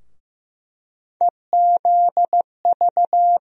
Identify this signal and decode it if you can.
Morse code, heard as a keyed tone, about 15 wpm, 700 Hz. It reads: EZV
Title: USA Call Sign Postfixes - 3 Letters 15wpm